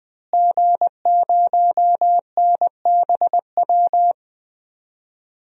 Morse code, heard as a keyed tone, about 21 wpm, 700 Hz. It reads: G0NBW
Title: DXCC-100 Call Signs - 5 Characters - Repeated 3x - 20wpm